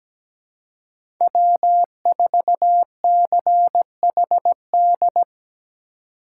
Morse code, heard as a keyed tone, about 17 wpm, 700 Hz. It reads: W4CHD
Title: USA Call Signs - 5 Characters - 3x Character Spacing - 17wpm